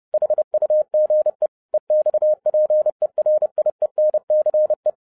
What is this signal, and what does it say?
Morse code, heard as a keyed tone, about 30 wpm, 610 Hz. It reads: HUGE EXPERIENCE
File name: 2011-07-mp3 / n55 BACK FROM A BIG ACCIDENT 30wpm